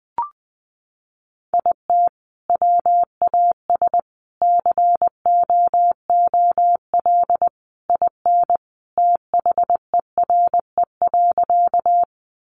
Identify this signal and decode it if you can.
Morse code, heard as a keyed tone, about 20 wpm, 700 Hz. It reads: IT WAS COOL IN THERE.
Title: Sentences from Top 1200 Words 20wpm